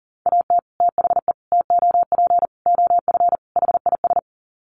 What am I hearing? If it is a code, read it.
Morse code, heard as a keyed tone, about 40 wpm, 700 Hz. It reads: AT THE TOP OF HIS
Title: Sets of 5 Words Encore 40wpm